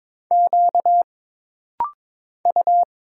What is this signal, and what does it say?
Morse code, heard as a keyed tone, about 23 wpm, 700 Hz. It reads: Q U
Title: Single Letter-Number Alt - Rapid-Fire 22wpm